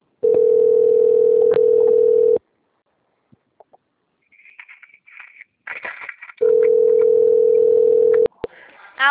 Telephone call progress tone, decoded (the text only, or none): ringback tone